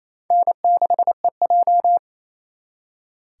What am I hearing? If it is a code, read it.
Morse code, heard as a keyed tone, about 35 wpm, 700 Hz. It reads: N6EJ
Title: Speed Racing - USA Call Signs 28wpm